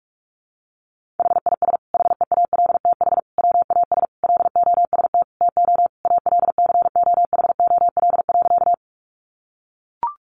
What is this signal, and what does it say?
Morse code, heard as a keyed tone, about 44 wpm, 700 Hz. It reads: HIS HEALTH WAS LOST TO ALCOHOL.
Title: Sentences from Top 1400 Words 45wpm